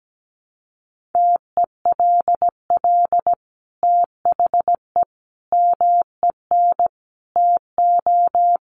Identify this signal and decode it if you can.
Morse code, heard as a keyed tone, about 17 wpm, 700 Hz. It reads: TELL THE MEN TO
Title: Sentences from Top 1400 Words 17wpm